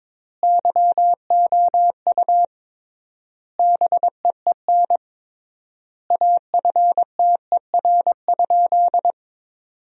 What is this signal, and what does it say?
Morse code, heard as a keyed tone, about 22 wpm, 700 Hz. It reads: YOU BEEN AFTER?
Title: 3x Word-Spacing - Sentences from Top 200 Words - 22wpm